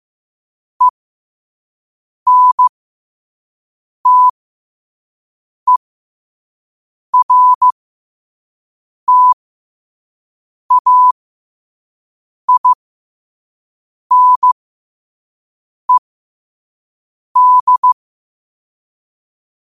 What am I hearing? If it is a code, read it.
Morse code, heard as a keyed tone, about 15 wpm, spaced as for 5 wpm, 1000 Hz. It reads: ENTERTAINED
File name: qotdfiles / 5wpm